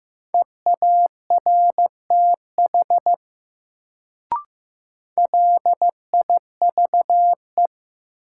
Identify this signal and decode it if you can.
Morse code, heard as a keyed tone, about 15 wpm, 700 Hz. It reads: EARTH LIVE
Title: Sentences from Top 200 Words - New Word Drill - 15wpm